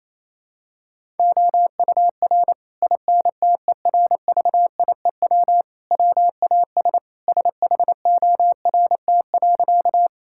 Morse code, heard as a keyed tone, about 28 wpm, 700 Hz. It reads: OUR INTERVIEW WAS SHORT.
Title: Sentences from Top 1600 Words 28wpm